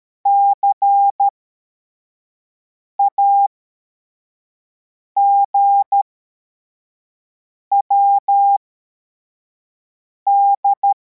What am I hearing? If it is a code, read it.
Morse code, heard as a keyed tone, about 13 wpm, spaced as for 4 wpm, 800 Hz. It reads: CAGWD
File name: formation / cw / Text63